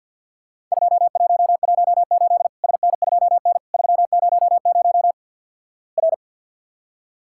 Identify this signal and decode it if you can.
Morse code, heard as a keyed tone, about 50 wpm, 700 Hz. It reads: 2118 SN2N 300 R
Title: Worked All Europe DX Contest - QTC Exchange with Repeats - 50wpm